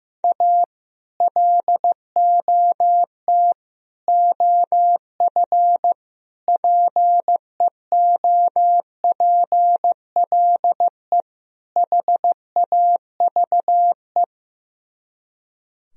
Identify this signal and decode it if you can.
Morse code, heard as a keyed tone, about 15 wpm, 700 Hz. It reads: A LOT OF PEOPLE HAVE
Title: Speed-Racing - Sets of 5 Words 15wpm